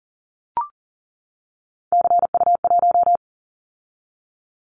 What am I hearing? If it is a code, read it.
Morse code, heard as a keyed tone, about 40 wpm, 700 Hz. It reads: CU1
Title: Callsign Prefixes for Top 50 DXCC - 40wpm